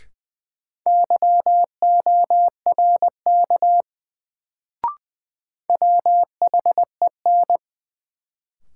Morse code, heard as a keyed tone, about 20 wpm, 700 Hz. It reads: YORK WHEN
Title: Sets of 2 Words - New Word Drill - 20wpm